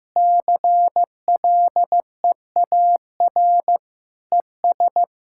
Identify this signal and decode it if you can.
Morse code, heard as a keyed tone, about 15 wpm, 700 Hz. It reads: CLEAR ES
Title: Practice QSO - Exchange 2 - Info - 15wpm